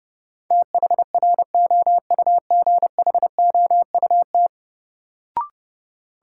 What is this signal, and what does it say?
Morse code, heard as a keyed tone, about 30 wpm, 700 Hz. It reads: THROUGHOUT
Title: Sentences from Top 800 Words - New Word Drill - 30wpm